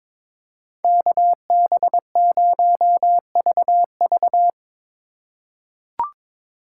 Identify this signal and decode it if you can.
Morse code, heard as a keyed tone, about 22 wpm, 700 Hz. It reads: KB0VV
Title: USA Call Signs - 5 Characters - 2x Character Spacing - 22wpm